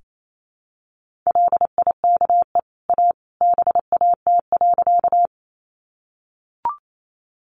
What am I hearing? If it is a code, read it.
Morse code, heard as a keyed tone, about 28 wpm, 700 Hz. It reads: LIKE A BAT.
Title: Sentences from Top 1500 Words 28wpm